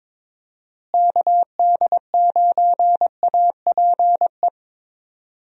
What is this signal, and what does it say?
Morse code, heard as a keyed tone, about 22 wpm, 700 Hz. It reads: KD9APE
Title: USA Call Signs - 6 Characters - Repeated 2x - 22wpm